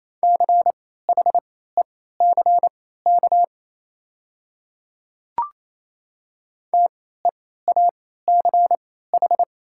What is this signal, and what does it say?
Morse code, heard as a keyed tone, about 28 wpm, spaced as for 16 wpm, 690 Hz. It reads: CHECK TEACH
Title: 3x Character Spacing - Top 500 Words - 28wpm